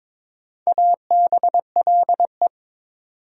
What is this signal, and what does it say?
Morse code, heard as a keyed tone, about 22 wpm, 700 Hz. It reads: ABLE
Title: Sentences from Top 700 Words - New Word Drill - 22wpm